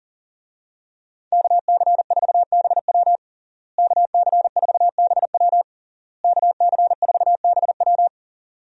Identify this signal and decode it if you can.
Morse code, heard as a keyed tone, about 40 wpm, 700 Hz. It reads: KC4BW KC4BW KC4BW
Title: USA Call Signs - 5 Characters - Repeated 3x - 40wpm